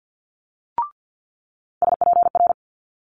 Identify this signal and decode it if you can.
Morse code, heard as a keyed tone, about 50 wpm, 700 Hz. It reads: SPR
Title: Multi-letter Phonemes Quick Fire 50wpm